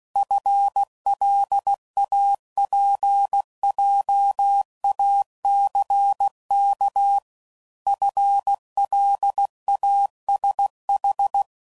Morse code, heard as a keyed tone, about 16 wpm, 800 Hz. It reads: FLAPJACK FLASH
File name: Prefixes / FLs & THs / FL Runs